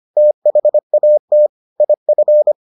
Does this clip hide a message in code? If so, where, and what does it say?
Yes: Morse code from 0.2 to 2.5 s, heard as a keyed tone, about 25 wpm, 600 Hz. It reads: THAT IF